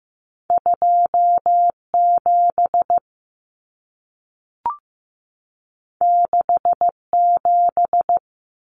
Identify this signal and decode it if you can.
Morse code, heard as a keyed tone, about 15 wpm, 700 Hz. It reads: 27 67
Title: Numbers - 2-Digit 15wpm